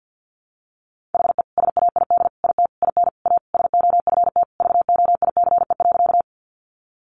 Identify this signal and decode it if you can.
Morse code, heard as a keyed tone, about 50 wpm, 700 Hz. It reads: HE SAID IT IN A SOFT VOICE.
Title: Sentences from Top 1500 Words 50wpm